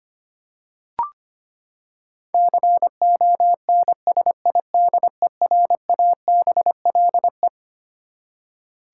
Morse code, heard as a keyed tone, about 25 wpm, 700 Hz. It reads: CONSIDERABLE
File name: Twelve Letter Words - 25wpm